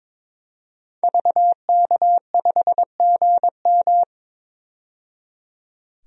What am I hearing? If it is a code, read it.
Morse code, heard as a keyed tone, about 22 wpm, 700 Hz. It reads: VK5GM
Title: DXCC-100 Call Signs - 5 Characters - 22wpm